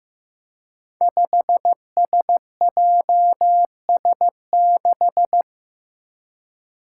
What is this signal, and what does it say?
Morse code, heard as a keyed tone, about 15 wpm, 700 Hz. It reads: 5SJS6
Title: Five Challenging Letter-Number Combinations 15wpm